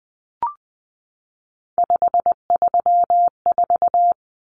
Morse code, heard as a keyed tone, about 20 wpm, 700 Hz. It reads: 534
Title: Numbers - 3-Digit 20wpm